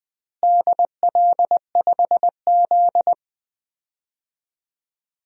Morse code, heard as a keyed tone, about 20 wpm, 700 Hz. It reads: DL5Z
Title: DXCC-100 Call Signs - 4 Characters - Repeated 2x - 20wpm